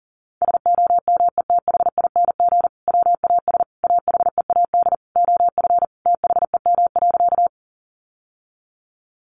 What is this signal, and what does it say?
Morse code, heard as a keyed tone, about 37 wpm, 700 Hz. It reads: SOMETHING WAS AHEAD OF THEM.
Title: Sentences from Top 1100 Words 40wpm